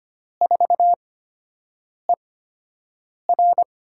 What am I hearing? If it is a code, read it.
Morse code, heard as a keyed tone, about 25 wpm, spaced as for 7 wpm, 700 Hz. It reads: 4ER
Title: USA Call Signs - 8x Character Spacing - 25wpm